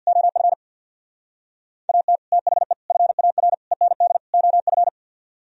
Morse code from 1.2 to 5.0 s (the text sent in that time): AT THE FAR END OF